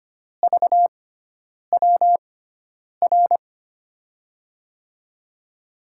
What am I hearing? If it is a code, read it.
Morse code, heard as a keyed tone, about 25 wpm, spaced as for 9 wpm, 700 Hz. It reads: VWR
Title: USA Call Signs - 6 Characters - 6x Character Spacing - 25wpm